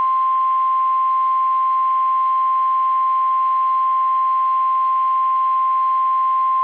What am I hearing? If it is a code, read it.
Emergency Alert System (SAME) tones (the 1050 Hz attention tone). It is heard from 1.0 to 6.0 s.
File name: eas / KSDZKDJL / KSDZKDJL